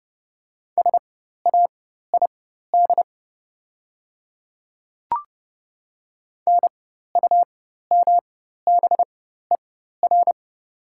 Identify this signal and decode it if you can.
Morse code, heard as a keyed tone, about 30 wpm, spaced as for 14 wpm, 700 Hz. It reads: SAID NUMBER
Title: 4x Character Spacing - Top 100 Words - 30wpm